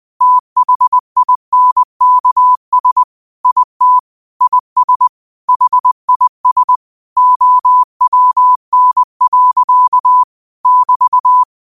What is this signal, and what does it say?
Morse code, heard as a keyed tone, about 20 wpm, 1000 Hz. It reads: THINKS IT IS HIS OWN. <BT>
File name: qotdfiles / 20wpm